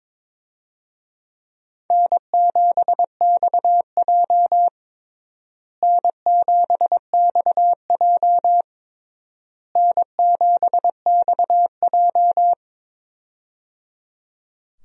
Morse code, heard as a keyed tone, about 22 wpm, 700 Hz. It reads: N7XJ N7XJ N7XJ
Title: USA Call Signs - 4 Characters - Repeated 3x - 22wpm